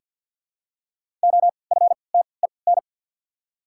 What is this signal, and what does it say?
Morse code, heard as a keyed tone, about 54 wpm, spaced as for 29 wpm, 700 Hz. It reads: OFTEN
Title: 3x Character Spacing - Top 500 Words - 50wpm